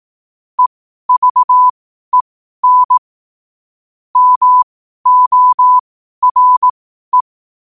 Morse code, heard as a keyed tone, about 18 wpm, spaced as for 12 wpm, 1000 Hz. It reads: EVEN MORE
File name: qotdfiles / 12wpm